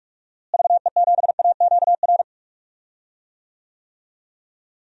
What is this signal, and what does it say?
Morse code, heard as a keyed tone, about 46 wpm, 700 Hz. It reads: VE7AQR
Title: DXCC-100 Call Signs - 6 Characters - Repeated 2x - 45wpm